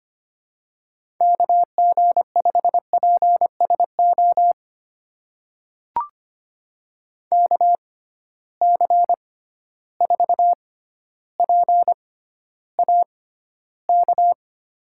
Morse code, heard as a keyed tone, about 25 wpm, spaced as for 13 wpm, 700 Hz. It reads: KG5PSO KC4PAK
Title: USA Call Signs - 6 Characters - 6x Character Spacing - 25wpm